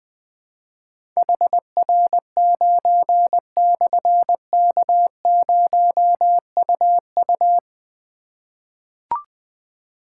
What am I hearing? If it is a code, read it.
Morse code, heard as a keyed tone, about 20 wpm, 700 Hz. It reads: HR9/K0UU
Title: Difficult Call Signs 20wpm